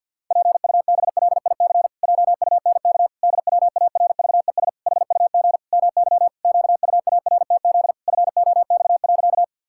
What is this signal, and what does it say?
Morse code, heard as a keyed tone, about 52 wpm, 700 Hz. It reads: PUBLIC JUNK DWARVES HUG MY QUARTZ FOX.
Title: Pangram Challenge v2 - 50wpm